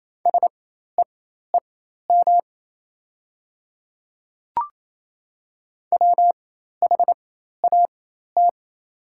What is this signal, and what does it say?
Morse code, heard as a keyed tone, about 28 wpm, spaced as for 13 wpm, 700 Hz. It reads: SEEM WHAT
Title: 4x Character Spacing - Top 300 Words - 28wpm